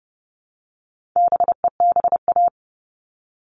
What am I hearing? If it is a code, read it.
Morse code, heard as a keyed tone, about 30 wpm, 700 Hz. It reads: BEBA
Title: All Cut Numbers - 4-Digit 30wpm